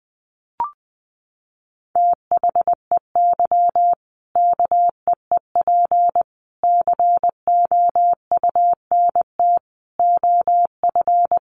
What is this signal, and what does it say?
Morse code, heard as a keyed tone, about 20 wpm, 700 Hz. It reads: THEY KEEP COUNT OF
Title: Sentences from Top 1500 Words 20wpm